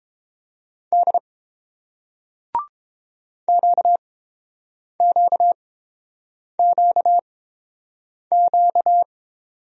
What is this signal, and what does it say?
Morse code, heard as a keyed tone, about 30 wpm, spaced as for 7 wpm, 700 Hz. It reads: D QQQQ